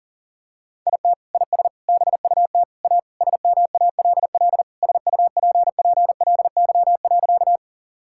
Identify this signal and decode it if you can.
Morse code, heard as a keyed tone, about 40 wpm, 700 Hz. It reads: IT IS BUT A SMALL SUPPLY.
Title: Sentences from Top 1000 Words - Review Part 5 - 40wpm